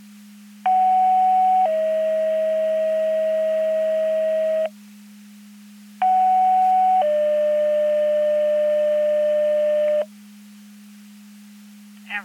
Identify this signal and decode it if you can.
Two-tone page, heard as a one-second tone, then a three-second tone, about 750 then 640 Hz, about 750 then 600 Hz. It is heard from 0.7 to 10.0 s.
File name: Portage 154.385 Paging